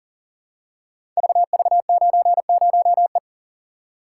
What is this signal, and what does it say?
Morse code, heard as a keyed tone, about 43 wpm, 700 Hz. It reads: VV90E
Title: Code Groups - 5 Letter:Numbers - 40wpm